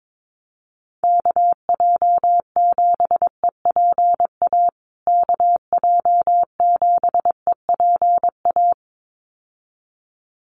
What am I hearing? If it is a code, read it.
Morse code, heard as a keyed tone, about 22 wpm, 700 Hz. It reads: KJ7EPA KJ7EPA
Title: USA Call Signs Repeated 2x 22wpm